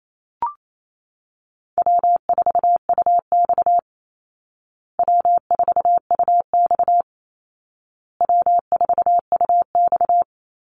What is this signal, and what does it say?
Morse code, heard as a keyed tone, about 28 wpm, 700 Hz. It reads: W4UX W4UX W4UX
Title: Easy Call Signs Repeated 3x - 28wpm